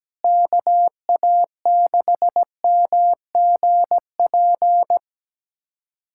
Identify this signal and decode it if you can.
Morse code, heard as a keyed tone, about 17 wpm, 700 Hz. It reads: KA6MGP
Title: USA Callsign DE Callsign - 6 Characters - 17wpm